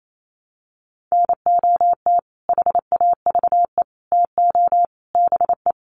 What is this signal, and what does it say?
Morse code, heard as a keyed tone, about 28 wpm, 700 Hz. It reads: NOT HAVE TO BE
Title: Sets of 4 Words 28wpm